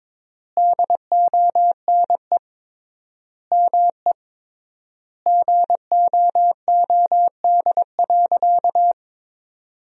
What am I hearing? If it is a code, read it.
Morse code, heard as a keyed tone, about 22 wpm, 700 Hz. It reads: DONE ME GOOD.
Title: 3x Word-Spacing - Sentences from Top 500 Words - 22wpm